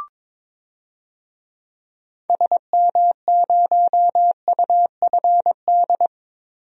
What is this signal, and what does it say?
Morse code, heard as a keyed tone, about 22 wpm, 700 Hz. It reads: SM0UFD
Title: DXCC-100 Call Signs - 6 Characters - Repeated 3x - 22wpm